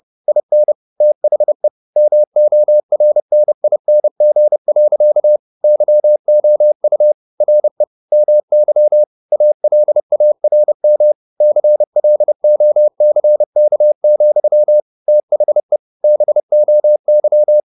Morse code, heard as a keyed tone, about 30 wpm, 600 Hz. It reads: IN THE MORNING. YOU RE MY ALARM CLOCK, THE BOY
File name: the old man and the sea / the old man and the sea 1 30wpm